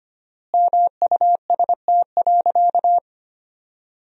Morse code, heard as a keyed tone, about 25 wpm, 700 Hz. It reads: MUST.
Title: 4x Word-Spacing - Sentences from Top 200 Words - 25wpm